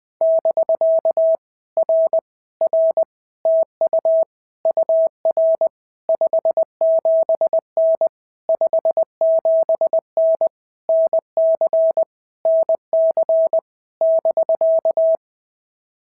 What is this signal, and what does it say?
Morse code, heard as a keyed tone, about 20 wpm, 650 Hz. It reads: <BK> R R TU UR 57N 57N NC NC <BK>
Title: POTA QSO Practice - 20wpm